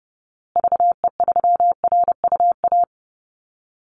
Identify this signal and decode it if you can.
Morse code, heard as a keyed tone, about 30 wpm, 700 Hz. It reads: VE3RUA
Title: Call Signs Repeated 2x - 30wpm